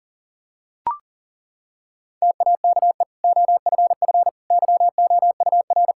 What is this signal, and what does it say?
Morse code, heard as a keyed tone, about 40 wpm, 700 Hz. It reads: TAKE OFF YOUR
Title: Sentences from Top 1600 Words 40wpm